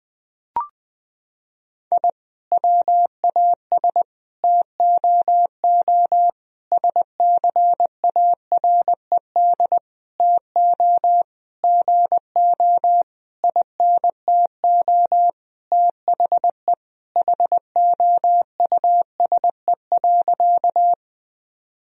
Morse code, heard as a keyed tone, about 20 wpm, 700 Hz. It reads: I WAS TOO SCARED TO GO INTO THE HOUSE.
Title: Sentences from Top 1000 Words - Review Part 4 - 20wpm